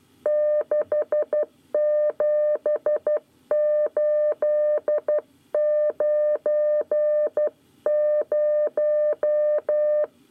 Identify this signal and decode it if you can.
Morse code, heard as a keyed tone, about 12 wpm, 600 Hz. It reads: 67890
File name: 9 lezione cw 1